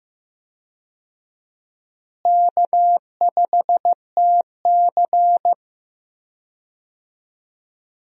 Morse code, heard as a keyed tone, about 15 wpm, 700 Hz. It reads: K5TC
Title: USA Call Signs - 4 Characters - Repeated 2x - 15wpm